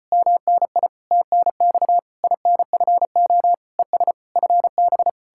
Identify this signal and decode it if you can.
Morse code, heard as a keyed tone, about 37 wpm, 700 Hz. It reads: MNI TNX INFO ES FB